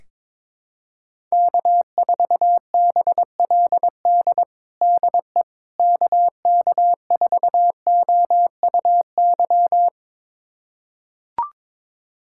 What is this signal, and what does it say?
Morse code, heard as a keyed tone, about 22 wpm, 700 Hz. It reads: K4BLD DE KK4OUY
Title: USA Callsign DE Callsign 22wpm